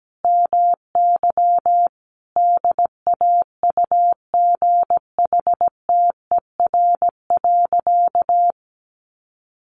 Morse code, heard as a keyed tone, about 17 wpm, 700 Hz. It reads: MY DAUGHTER.
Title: Sentences from Top 1500 Words 17wpm